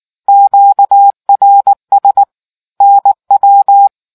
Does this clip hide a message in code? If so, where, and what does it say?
Yes: Morse code from 0.3 to 3.9 s, heard as a keyed tone, about 19 wpm, 800 Hz. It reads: QRS NW